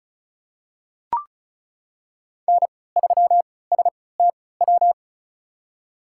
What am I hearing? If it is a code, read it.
Morse code, heard as a keyed tone, about 34 wpm, spaced as for 20 wpm, 700 Hz. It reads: N3STW
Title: USA Call Signs - 5 Characters - 3x Character Spacing - 35wpm